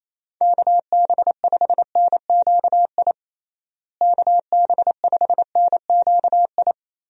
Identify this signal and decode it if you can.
Morse code, heard as a keyed tone, about 28 wpm, 700 Hz. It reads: KB5NQI KB5NQI
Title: USA Call Signs - 6 Characters - Repeated 2x - 28wpm